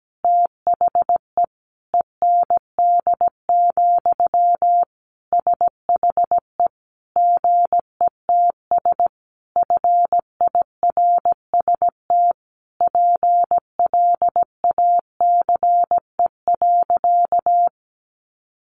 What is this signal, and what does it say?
Morse code, heard as a keyed tone, about 17 wpm, 700 Hz. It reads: THE END, SHE GETS FIRST PLACE.
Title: Additional Easy Phrases 17wpm